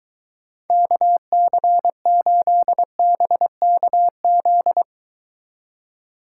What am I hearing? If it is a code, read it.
Morse code, heard as a keyed tone, about 23 wpm, 700 Hz. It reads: KC8BKZ